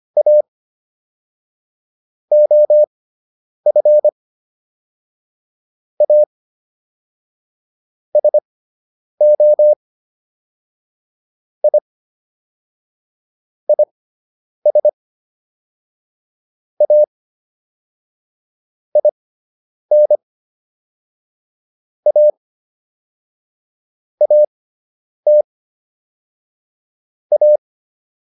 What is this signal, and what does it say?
Morse code, heard as a keyed tone, about 25 wpm, spaced as for 9 wpm, 600 Hz. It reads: A OF A SO I IS A IN A AT A